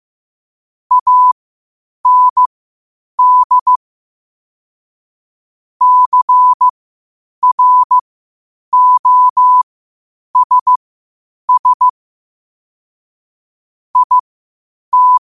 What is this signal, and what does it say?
Morse code, heard as a keyed tone, about 15 wpm, spaced as for 8 wpm, 1000 Hz. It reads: AND CROSS IT
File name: qotdfiles / 8wpm